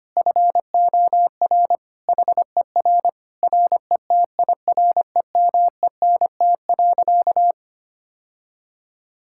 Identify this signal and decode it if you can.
Morse code, heard as a keyed tone, about 25 wpm, 700 Hz. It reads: FOR HER RETIREMENT.
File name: practice / 25wpm / Difficult Sentences with Hints - 25wpm